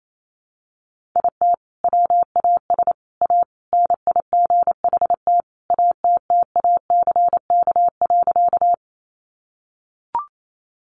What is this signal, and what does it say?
Morse code, heard as a keyed tone, about 28 wpm, 700 Hz. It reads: IT WAS A NIGHT ATTACK.
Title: Sentences from Top 1500 Words 28wpm